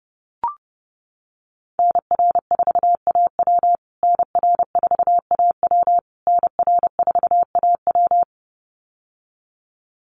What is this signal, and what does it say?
Morse code, heard as a keyed tone, about 30 wpm, 700 Hz. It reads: NR4AW NR4AW NR4AW
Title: USA Call Signs Repeated 3x 30wpm